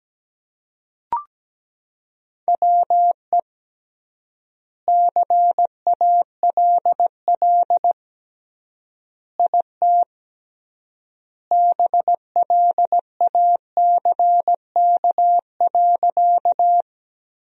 Morse code, heard as a keyed tone, about 17 wpm, 700 Hz. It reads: WE CALL IT BLACK.
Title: 3x Word-Spacing - Sentences from Top 400 Words - 17wpm